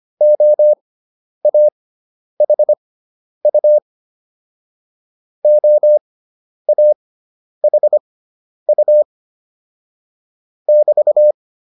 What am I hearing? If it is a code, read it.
Morse code, heard as a keyed tone, about 25 wpm, spaced as for 10 wpm, 600 Hz. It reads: OAHU OAHU <BT>
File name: QSO104 10